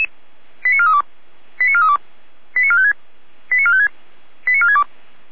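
Five-tone selective call (ZVEI), heard as quick runs of five tones, tones of about 70 ms. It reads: 89432 89432 89456 89456 89462
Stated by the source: ZVEI five-tone call received by SDR.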